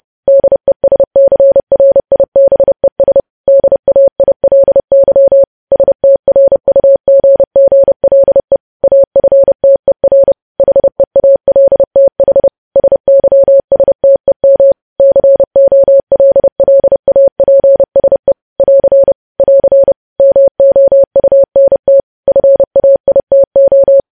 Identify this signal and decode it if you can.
Morse code, heard as a keyed tone, about 30 wpm, 550 Hz. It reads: DESCRIBES DAILY STRUGGLE AFTER HEALTH SYSTEM COLLAPSE <AR> <AR> MOUNT FAITO
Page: news headlines in morse code at 30 wpm